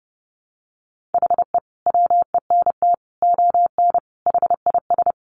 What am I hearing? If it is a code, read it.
Morse code, heard as a keyed tone, about 30 wpm, 700 Hz. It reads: HE WENT ON HIS
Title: Sentences from Top 200 Words 30wpm